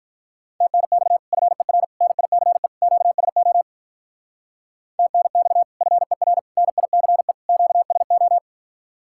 Morse code, heard as a keyed tone, about 46 wpm, 700 Hz. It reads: TNX FER NICE QSO TNX FER NICE QSO